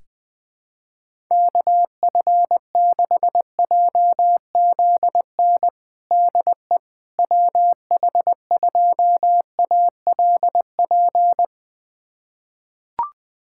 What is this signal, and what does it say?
Morse code, heard as a keyed tone, about 20 wpm, 700 Hz. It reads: KF6JZN DE WH2ALP
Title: USA Callsign DE Callsign - 6 Characters - 20wpm